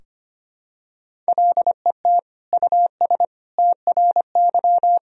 Morse code, heard as a keyed tone, about 25 wpm, 700 Hz. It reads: LET US TRY
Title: Sentences from Top 1000 Words - Review Part 4 - 25wpm